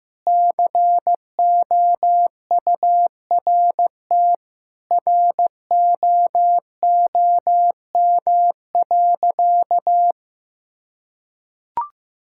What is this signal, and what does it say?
Morse code, heard as a keyed tone, about 15 wpm, 700 Hz. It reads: COURT ROOM.
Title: Sentences from Top 1600 Words 15wpm